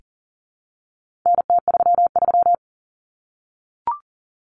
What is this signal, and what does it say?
Morse code, heard as a keyed tone, about 40 wpm, 700 Hz. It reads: NT33
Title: Common Cut Numbers - 4-Digit 40wpm